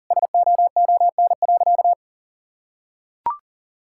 Morse code, heard as a keyed tone, about 40 wpm, 700 Hz. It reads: SOON.
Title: Sentences from Top 1600 Words 40wpm